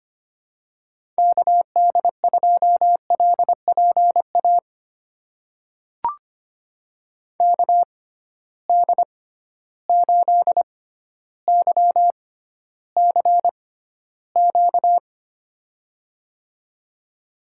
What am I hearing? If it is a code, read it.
Morse code, heard as a keyed tone, about 25 wpm, spaced as for 13 wpm, 700 Hz. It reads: KD2LPA KD8YCQ
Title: USA Call Signs - 6 Characters - 6x Character Spacing - 25wpm